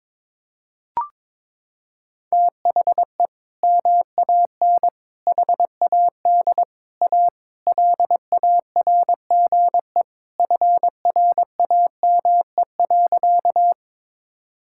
Morse code, heard as a keyed tone, about 21 wpm, 700 Hz. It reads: THE MAN HAD A LARGE FRAME.
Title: Sentences from Top 1000 Words - Review Part 6 - 22wpm